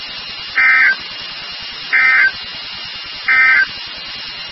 Emergency Alert System (SAME) tones, heard as harsh buzzing bursts. It is heard from 0.5 to 3.7 s.
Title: NOAA Weather Radio WXK36